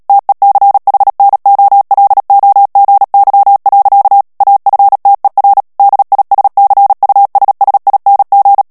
Morse code, heard as a keyed tone, about 37 wpm, 800 Hz. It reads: TECHNOLOGY. AFTER DISCUSSING
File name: morse mp3 / ANGLAIS texte clair 3 vitesse 35 mots minute